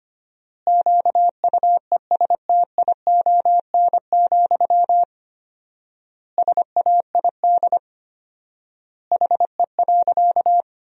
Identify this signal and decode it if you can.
Morse code, heard as a keyed tone, about 25 wpm, 700 Hz. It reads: QUESTION, SAID HE.
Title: 4x Word-Spacing - Sentences from Top 400 Words - 25wpm